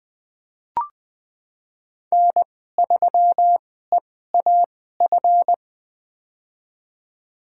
Morse code, heard as a keyed tone, about 20 wpm, spaced as for 14 wpm, 700 Hz. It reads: N3EAF
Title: USA Call Signs - 5 Characters - 2x Character Spacing - 20wpm